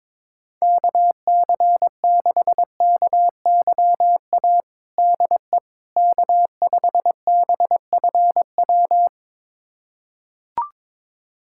Morse code, heard as a keyed tone, about 22 wpm, 700 Hz. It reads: KC6KYA DE K5BFW
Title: USA Callsign DE Callsign 22wpm